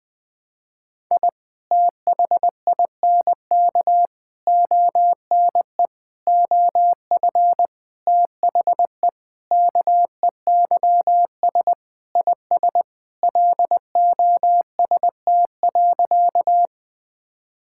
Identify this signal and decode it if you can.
Morse code, heard as a keyed tone, about 20 wpm, 700 Hz. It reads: I THINK ONE OF THE KEYS IS LOST.
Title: Sentences with 5-Word Phrases - Spoken Beginning - 20wpm